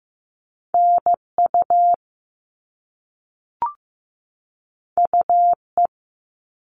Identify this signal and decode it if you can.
Morse code, heard as a keyed tone, about 15 wpm, 700 Hz. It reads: NU UE
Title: All Cut Numbers - 2-Digit 15wpm